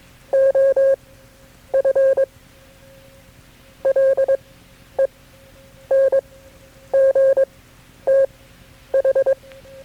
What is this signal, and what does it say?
Morse code, heard as a keyed tone, about 22 wpm, spaced as for 10 wpm, 540 Hz. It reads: OF LENGTH
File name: KB6MT-tape3-side1